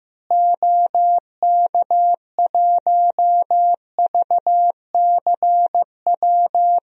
Morse code, heard as a keyed tone, about 15 wpm, 700 Hz. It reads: OK1VCW
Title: DXCC-100 Call Signs - 6 Characters - Repeated 3x - 15wpm